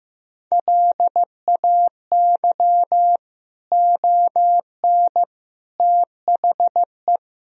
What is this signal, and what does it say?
Morse code, heard as a keyed tone, about 15 wpm, 700 Hz. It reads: LAY ON THE
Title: Sentences from Top 1000 Words - Review Part 2 - 15wpm